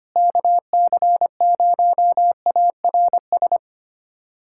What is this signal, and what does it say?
Morse code, heard as a keyed tone, about 25 wpm, 700 Hz. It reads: KC0ARS
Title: USA Callsign DE Callsign - 6 Characters - 25wpm